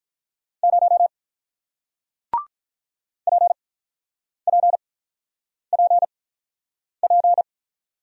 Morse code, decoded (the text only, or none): 0 PPPP